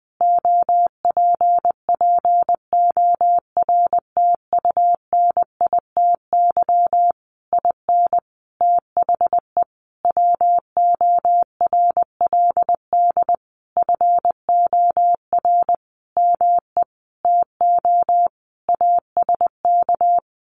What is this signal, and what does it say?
Morse code, heard as a keyed tone, about 20 wpm, 700 Hz. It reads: OPPORTUNITY IN THE WORLD FOR ME TO ASK
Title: Sentences from Top 1500 Words 20wpm